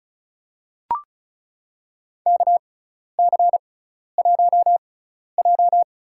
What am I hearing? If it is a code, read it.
Morse code, heard as a keyed tone, about 34 wpm, spaced as for 12 wpm, 700 Hz. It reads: KC1J
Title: USA Call Signs - 4 Characters - 6x Character Spacing - 35wpm